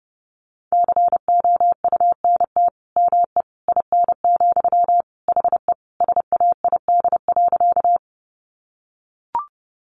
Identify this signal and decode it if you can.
Morse code, heard as a keyed tone, about 28 wpm, 700 Hz. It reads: COUNT ME IN, HE SAID.
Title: Sentences from Top 1500 Words 30wpm